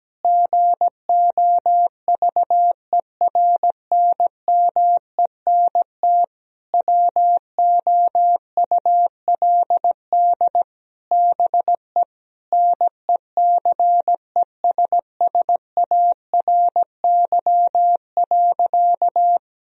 Morse code, heard as a keyed tone, about 17 wpm, 700 Hz. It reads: GOVERNMENT WOULD BE NECESSARY.
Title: Sentences from Top 1000 Words - Review Part 3 - 17wpm